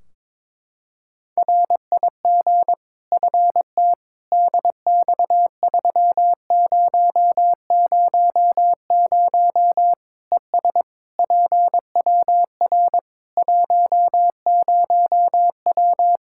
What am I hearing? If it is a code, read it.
Morse code, heard as a keyed tone, about 21 wpm, 700 Hz. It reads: RIG FT DX3000 ES PWR 10W
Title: Practice QSO - Exchange 2 - Info - 22wpm